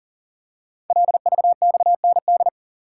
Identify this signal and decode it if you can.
Morse code, heard as a keyed tone, about 41 wpm, 700 Hz. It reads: LVXND
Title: Code Groups - 5 Letters - 40wpm